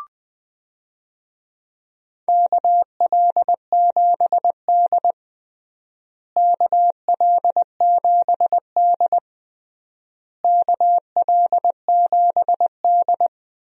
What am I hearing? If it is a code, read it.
Morse code, heard as a keyed tone, about 20 wpm, 700 Hz. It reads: KL7D KL7D KL7D
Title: DXCC-100 Call Signs - 4 Characters - Repeated 3x - 20wpm